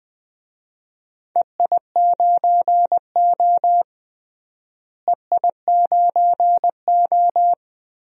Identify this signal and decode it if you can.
Morse code, heard as a keyed tone, about 20 wpm, 700 Hz. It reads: EI9O EI9O
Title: DXCC-100 Call Signs - 4 Characters - Repeated 3x - 20wpm